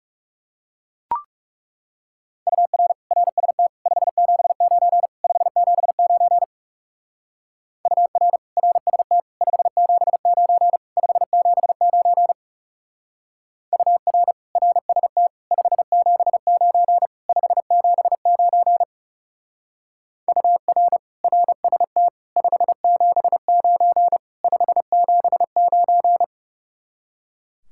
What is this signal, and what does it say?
Morse code, heard as a keyed tone, about 36 wpm, 700 Hz. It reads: UR RST 579 579 UR RST 579 579 UR RST 579 579 UR RST 579 579
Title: Speed-Racing Common QSO Phrases Encore - 30wpm